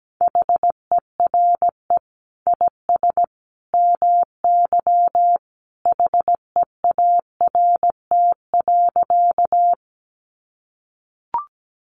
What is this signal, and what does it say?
Morse code, heard as a keyed tone, about 17 wpm, 700 Hz. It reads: HERE IS MY HEART.
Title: Sentences from Top 1200 Words 17wpm